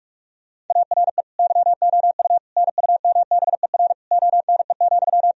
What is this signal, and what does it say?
Morse code, heard as a keyed tone, about 43 wpm, 700 Hz. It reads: ARE YOU NUMBER ONE,
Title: Sentences from Top 1000 Words - Review Part 1 - 45wpm